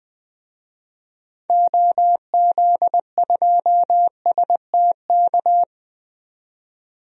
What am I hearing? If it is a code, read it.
Morse code, heard as a keyed tone, about 20 wpm, 700 Hz. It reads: OZ2STK
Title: DXCC-100 Call Signs - 6 Characters - 20wpm